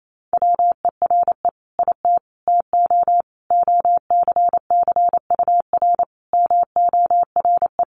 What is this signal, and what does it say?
Morse code, heard as a keyed tone, about 28 wpm, 700 Hz. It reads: WERE IT TO OCCUR MORE
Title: Sentences from Top 1300 Words 28wpm